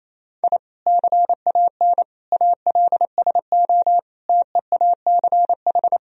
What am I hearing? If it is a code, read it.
Morse code, heard as a keyed tone, about 28 wpm, 700 Hz. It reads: I CAN ALSO TEACH
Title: Sentences from Top 1000 Words - Review Part 2 - 28wpm